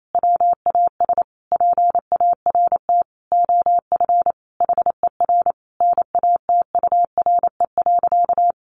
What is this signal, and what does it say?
Morse code, heard as a keyed tone, about 28 wpm, 700 Hz. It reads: WAS PART OF HER NATURE.
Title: Sentences from Top 1300 Words 28wpm